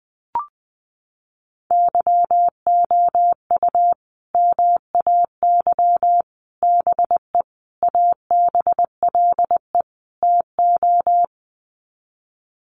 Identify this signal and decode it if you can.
Morse code, heard as a keyed tone, about 19 wpm, 700 Hz. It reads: YOU MAY BE ABLE TO
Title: Sets of 5 Words Encore 20wpm